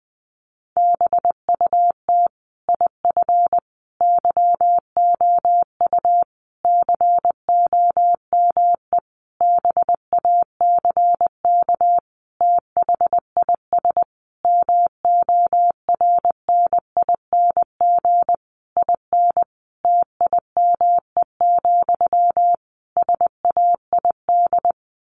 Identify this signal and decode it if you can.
Morse code, heard as a keyed tone, about 20 wpm, 700 Hz. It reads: BUT IF YOU COME BACK THIS MORNING IN TIME, SAID
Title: A Sound Of Thunder 20wpm